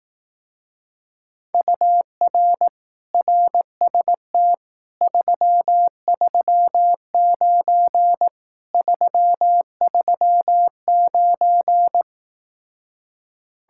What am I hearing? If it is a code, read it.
Morse code, heard as a keyed tone, about 18 wpm, 700 Hz. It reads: UR RST 339 339
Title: Speed-Racing Common QSO Phrases Encore - 15wpm